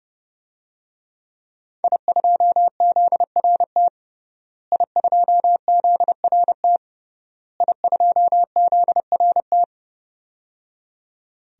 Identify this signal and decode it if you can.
Morse code, heard as a keyed tone, about 30 wpm, 700 Hz. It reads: I2ZRT I2ZRT I2ZRT
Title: DXCC-100 Call Signs - 5 Characters - Repeated 3x - 30wpm